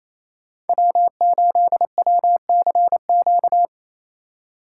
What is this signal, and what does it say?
Morse code, heard as a keyed tone, about 28 wpm, 700 Hz. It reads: W8WCQ
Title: USA Call Signs - 5 Characters - 4x Character Spacing - 28wpm